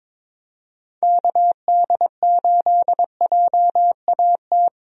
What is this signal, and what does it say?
Morse code, heard as a keyed tone, about 22 wpm, 700 Hz. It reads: KD8JAT
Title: USA Call Signs - 6 Characters - 22wpm